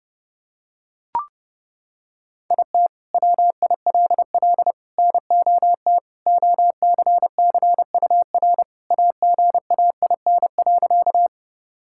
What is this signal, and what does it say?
Morse code, heard as a keyed tone, about 30 wpm, 700 Hz. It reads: IT WILL NOT OCCUR AGAIN.
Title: Sentences from Top 1000 Words - Review Part 6 - 30wpm